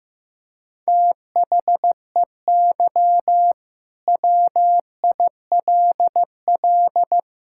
Morse code, heard as a keyed tone, about 16 wpm, 700 Hz. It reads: THEY WILL
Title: Sentences from Top 1000 Words - Review Part 5 - 15wpm